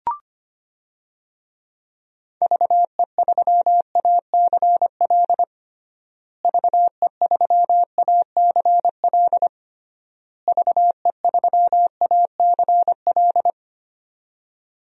Morse code, heard as a keyed tone, about 25 wpm, 700 Hz. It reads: VE3ACL VE3ACL VE3ACL
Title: DXCC-100 Call Signs - 6 Characters - Repeated 3x - 25wpm